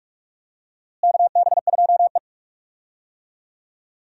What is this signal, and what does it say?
Morse code, heard as a keyed tone, about 46 wpm, 700 Hz. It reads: KB2E
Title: USA Call Signs - 4 Characters - Repeated 3x - 45wpm